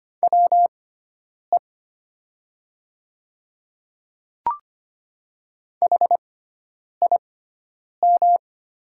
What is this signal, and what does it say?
Morse code, heard as a keyed tone, about 25 wpm, spaced as for 9 wpm, 690 Hz. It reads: WE HIM
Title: 6x Character Spacing - Top 100 Words - 25wpm